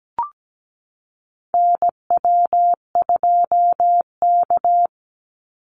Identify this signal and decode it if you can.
Morse code, heard as a keyed tone, about 17 wpm, 700 Hz. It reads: NW2K
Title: Call Signs Repeated 3x - 17wpm